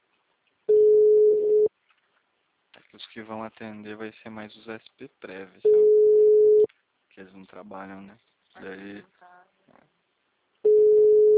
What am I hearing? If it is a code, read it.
Ringback tone (Europe, 425 Hz), with speech.